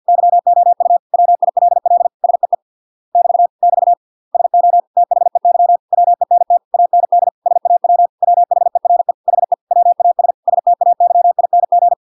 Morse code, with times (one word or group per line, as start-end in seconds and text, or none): 0.1-1.0 YOU
1.1-2.1 WILL
2.2-2.5 SEE
3.1-3.5 <BT>
3.6-3.9 <BT>
4.3-4.8 SO
5.0-5.8 THEY
5.9-6.6 WENT
6.7-7.3 AND
7.5-8.1 SAW
8.2-9.1 WHERE
9.3-9.5 HE
9.7-10.3 WAS
10.5-11.9 STAYING